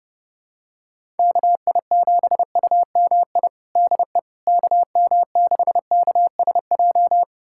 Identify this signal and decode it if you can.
Morse code, heard as a keyed tone, about 32 wpm, 700 Hz. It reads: KI7UMI DE KM6KSJ
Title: USA Callsign DE Callsign - 6 Characters - 30wpm